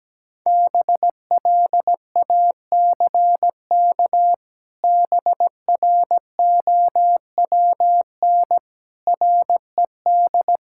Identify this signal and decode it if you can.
Morse code, heard as a keyed tone, about 17 wpm, 700 Hz. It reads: BLACK BROWN RED
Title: Pangram Challenge v2 - 17wpm